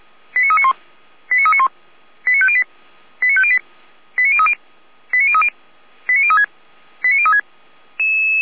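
Five-tone selective call (ZVEI), heard as quick runs of five tones, tones of about 70 ms. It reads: 89391 89391 89508 89508 89030 89030 89036 89036
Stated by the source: ZVEI five-tone selective call 89050 received with an SDR.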